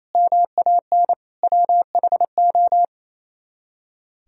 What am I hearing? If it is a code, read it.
Morse code, heard as a keyed tone, about 33 wpm, 700 Hz. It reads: MAN WHO
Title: Speed-Racing - Sets of 3 Words 28wpm